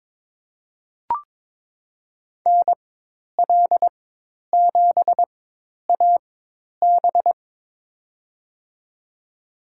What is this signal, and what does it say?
Morse code, heard as a keyed tone, about 23 wpm, spaced as for 10 wpm, 700 Hz. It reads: NL7AB
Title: USA Call Signs - 5 Characters - 4x Character Spacing - 22wpm